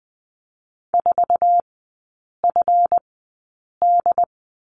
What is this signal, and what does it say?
Morse code, heard as a keyed tone, about 20 wpm, spaced as for 8 wpm, 700 Hz. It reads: 4FD
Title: 152 Lesson 25e - Letter p - Three Characters 20wpm